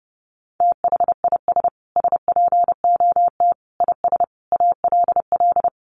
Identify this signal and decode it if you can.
Morse code, heard as a keyed tone, about 28 wpm, 700 Hz. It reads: THIS SPOT IS ALL
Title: Sentences from Top 1100 Words 30wpm